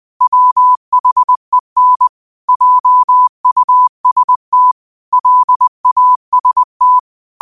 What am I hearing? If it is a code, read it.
Morse code, heard as a keyed tone, about 20 wpm, 1000 Hz. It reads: WHEN JUST LAST